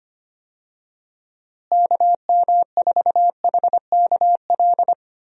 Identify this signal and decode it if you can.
Morse code, heard as a keyed tone, about 25 wpm, 700 Hz. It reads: KM4HKL
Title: USA Call Signs - 6 Characters - Repeated 2x - 25wpm